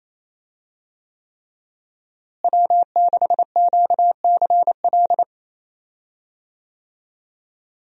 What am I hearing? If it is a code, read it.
Morse code, heard as a keyed tone, about 28 wpm, 700 Hz. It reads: W6QCL
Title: USA Call Signs - 5 Characters - Repeated 3x - 28wpm